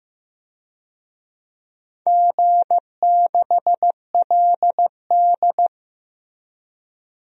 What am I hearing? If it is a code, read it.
Morse code, heard as a keyed tone, about 15 wpm, 700 Hz. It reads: G6LD
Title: DXCC-100 Call Signs - 4 Characters - Repeated 3x - 15wpm